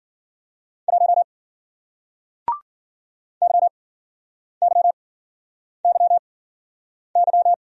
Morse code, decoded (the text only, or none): 1 YYYY